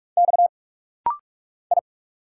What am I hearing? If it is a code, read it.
Morse code, heard as a keyed tone, about 46 wpm, 690 Hz. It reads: X I
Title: Single Letters - Rapid-Fire 45wpm